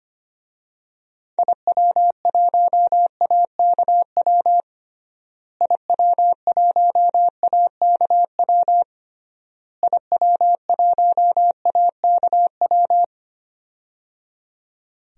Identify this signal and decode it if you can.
Morse code, heard as a keyed tone, about 25 wpm, 700 Hz. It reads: IW1AKW IW1AKW IW1AKW
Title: DXCC-100 Call Signs - 6 Characters - Repeated 3x - 25wpm